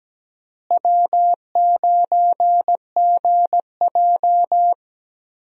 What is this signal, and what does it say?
Morse code, heard as a keyed tone, about 17 wpm, 700 Hz. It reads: W9GJ
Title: USA Call Signs - 4 Characters - Repeated 3x - 17wpm